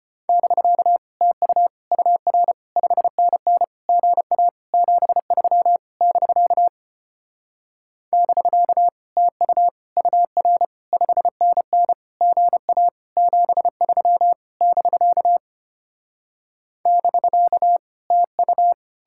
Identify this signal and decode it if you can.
Morse code, heard as a keyed tone, about 33 wpm, 700 Hz. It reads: <BK> TU UR 5NN GA 73 <BK> <BK> TU UR 5NN GA 73 <BK> <BK> TU